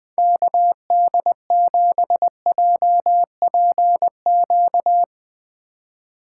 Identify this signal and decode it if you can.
Morse code, heard as a keyed tone, about 19 wpm, 700 Hz. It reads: KD7JPQ
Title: Speed Racing - USA 2x3 Call Signs 17wpm